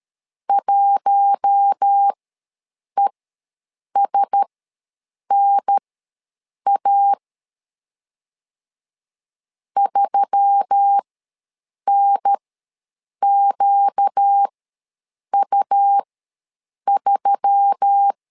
Morse code, heard as a keyed tone, about 13 wpm, spaced as for 7 wpm, 800 Hz. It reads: 1ESNA 3NQU3
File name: LEZIONI / 4 Q1S3 ws 12wpm 2038